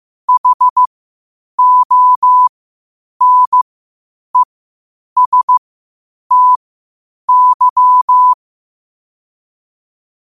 Morse code, heard as a keyed tone, about 15 wpm, spaced as for 8 wpm, 1000 Hz. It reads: HONESTY
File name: qotdfiles / 8wpm